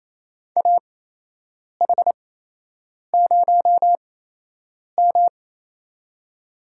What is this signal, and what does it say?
Morse code, heard as a keyed tone, about 28 wpm, spaced as for 8 wpm, 700 Hz. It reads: AH0M
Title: USA Call Signs - 4 Characters - 8x Character Spacing - 28wpm